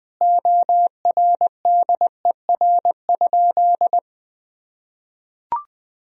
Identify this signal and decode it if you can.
Morse code, heard as a keyed tone, about 20 wpm, 700 Hz. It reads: ORDER?
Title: Sentences from Top 400 Words and Abbreviations 20wpm